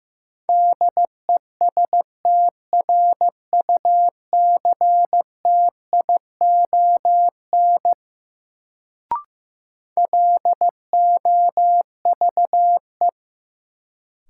Morse code, heard as a keyed tone, about 15 wpm, 700 Hz. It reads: DESTRUCTION LOVE
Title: Sets of 5 Words Encore - New Word Drill - 15wpm